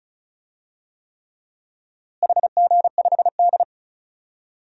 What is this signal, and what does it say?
Morse code, heard as a keyed tone, about 36 wpm, 700 Hz. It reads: HG5D
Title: DXCC-100 Call Signs - 4 Characters - Repeated 3x - 35wpm